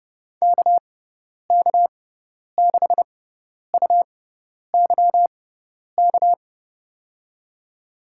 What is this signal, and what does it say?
Morse code, heard as a keyed tone, about 32 wpm, spaced as for 10 wpm, 700 Hz. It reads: KK6UYK
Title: USA Call Signs - 6x Character Spacing - 30wpm